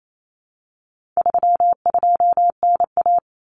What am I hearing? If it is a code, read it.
Morse code, heard as a keyed tone, about 28 wpm, 700 Hz. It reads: 32NA
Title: Common Cut Numbers - 4-Digit 28wpm